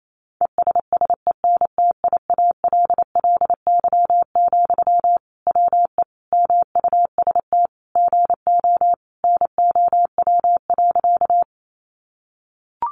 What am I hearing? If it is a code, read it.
Morse code, heard as a keyed tone, about 28 wpm, 700 Hz. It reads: ESSENTIALLY, WE MUST GO NOW.
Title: Sentences from Top 1300 Words 28wpm